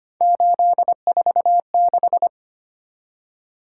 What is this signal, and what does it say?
Morse code, heard as a keyed tone, about 25 wpm, 700 Hz. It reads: 846
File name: Three Letter-Number Combinations - 25wpm